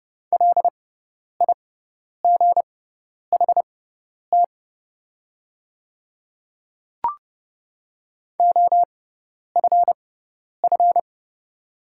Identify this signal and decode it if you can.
Morse code, heard as a keyed tone, about 30 wpm, spaced as for 10 wpm, 700 Hz. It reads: LIGHT OFF
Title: 6x Character Spacing - Top 300 Words - 30wpm